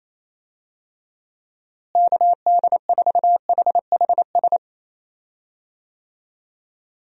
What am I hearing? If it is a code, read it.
Morse code, heard as a keyed tone, about 28 wpm, 700 Hz. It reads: KD4HHS
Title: USA Call Signs - 6 Characters - Repeated 2x - 28wpm